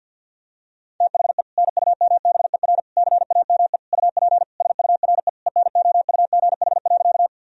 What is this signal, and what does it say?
Morse code, heard as a keyed tone, about 52 wpm, 700 Hz. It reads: THE NUMBER CAME UP SURE ENOUGH.
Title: Sentences from Top 1000 Words - Review Part 1 - 50wpm